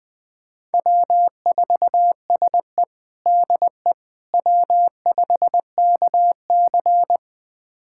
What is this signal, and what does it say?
Morse code, heard as a keyed tone, about 20 wpm, 700 Hz. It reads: W4SE DE W5KC
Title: USA Callsign DE Callsign - 4 Characters - 20wpm